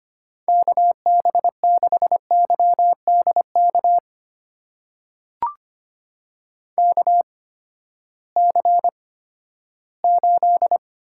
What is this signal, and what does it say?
Morse code, heard as a keyed tone, about 25 wpm, spaced as for 14 wpm, 700 Hz. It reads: KB6YDK KC8
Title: USA Call Signs - 6 Characters - 8x Character Spacing - 25wpm